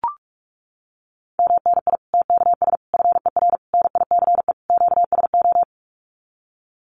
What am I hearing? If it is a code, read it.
Morse code, heard as a keyed tone, about 44 wpm, 700 Hz. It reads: MNI TKS FER NICE QSO
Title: Common QSO Phrases 45wpm